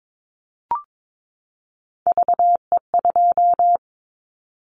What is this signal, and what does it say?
Morse code, heard as a keyed tone, about 22 wpm, 700 Hz. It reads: VE2
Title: Callsign Prefixes for Top 100 DXCC - 22wpm